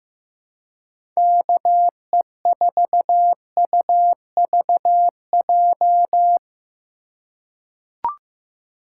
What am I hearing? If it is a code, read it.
Morse code, heard as a keyed tone, about 15 wpm, 700 Hz. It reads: KE4UVJ
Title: USA Call Signs - 4x Character Spacing - 15wpm